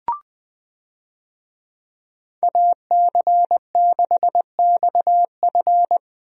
Morse code, heard as a keyed tone, about 20 wpm, 700 Hz. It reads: AC6XF
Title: USA Call Signs - 5 Characters - 20wpm